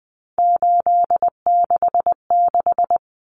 Morse code, heard as a keyed tone, about 20 wpm, 700 Hz. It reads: 866
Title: Common Cut Numbers - 3-Digit 20wpm